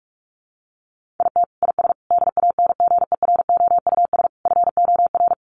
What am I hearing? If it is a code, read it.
Morse code, heard as a keyed tone, about 44 wpm, 700 Hz. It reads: IT IS DANGEROUS FOR
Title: Sentences from Top 1200 Words 45wpm